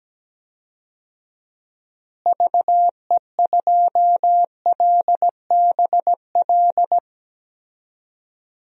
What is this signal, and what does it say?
Morse code, heard as a keyed tone, about 17 wpm, 700 Hz. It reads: VE2LBL
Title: DXCC-100 Call Signs - 6 Characters - Repeated 3x - 17wpm